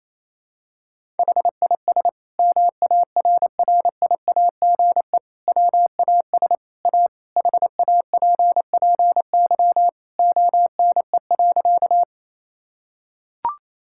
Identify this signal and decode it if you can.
Morse code, heard as a keyed tone, about 28 wpm, 700 Hz. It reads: HIS MARRIAGE WAS A HAPPY ONE.
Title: Sentences from Top 1000 Words - Review Part 5 - 28wpm